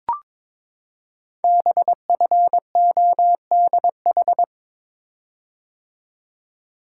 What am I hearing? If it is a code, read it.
Morse code, heard as a keyed tone, about 22 wpm, 700 Hz. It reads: BFODH
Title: Code Groups - 5 Letters - 22wpm